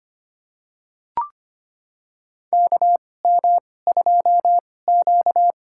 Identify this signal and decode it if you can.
Morse code, heard as a keyed tone, about 25 wpm, spaced as for 18 wpm, 700 Hz. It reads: KM2Q
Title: USA Call Signs - 4 Characters - 2x Character Spacing - 25wpm